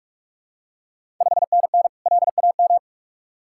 Morse code, heard as a keyed tone, about 45 wpm, 700 Hz. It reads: 5NN LAM
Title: Ontario QSO Party Exchange 45wpm